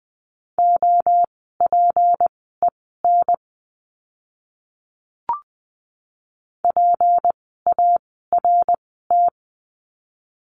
Morse code, heard as a keyed tone, about 20 wpm, spaced as for 14 wpm, 700 Hz. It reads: OPEN PART
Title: 153 Lesson 25f - Letter p - Words 20wpm